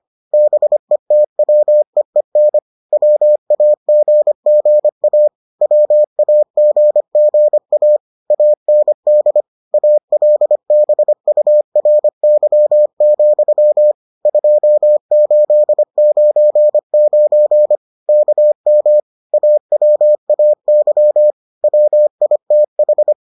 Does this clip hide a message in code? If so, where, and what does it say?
Yes: Morse code from 0.3 to 23.1 s, heard as a keyed tone, about 25 wpm, 600 Hz. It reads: BETWEEN WAGGA WAGGA AND ALBURY, 2899 KM AWAY WITH